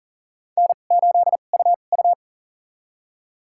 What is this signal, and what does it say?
Morse code, heard as a keyed tone, about 40 wpm, spaced as for 29 wpm, 700 Hz. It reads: N8UU
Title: USA Call Signs - 4 Characters - 2x Character Spacing - 40wpm